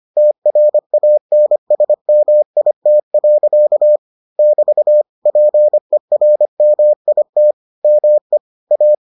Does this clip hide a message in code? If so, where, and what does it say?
Yes: Morse code from 0.2 to 8.9 s, heard as a keyed tone, about 25 wpm, 600 Hz. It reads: TRANSMIT. <BT> PERMIT ME A